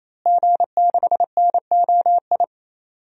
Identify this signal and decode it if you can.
Morse code, heard as a keyed tone, about 28 wpm, 700 Hz. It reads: G6NOI
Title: DXCC-100 Call Signs - 5 Characters - Repeated 3x - 28wpm